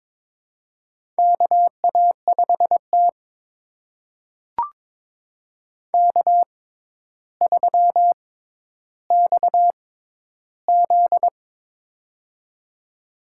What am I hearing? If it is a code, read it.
Morse code, heard as a keyed tone, about 22 wpm, spaced as for 11 wpm, 700 Hz. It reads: KA5T K3XZ
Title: USA Call Signs - 4 Characters - 6x Character Spacing - 22wpm